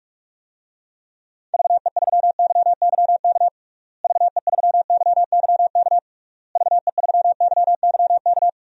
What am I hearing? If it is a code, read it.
Morse code, heard as a keyed tone, about 46 wpm, 700 Hz. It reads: VE3YYK VE3YYK VE3YYK
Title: DXCC-100 Call Signs - 6 Characters - Repeated 3x - 45wpm